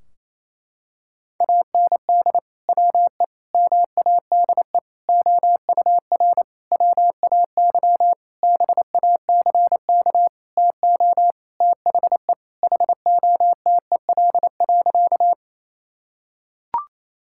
Morse code, heard as a keyed tone, about 28 wpm, 700 Hz. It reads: AND WE MADE OUR WAY BACK TO THE HOTEL.
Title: Sentences from Top 1600 Words 28wpm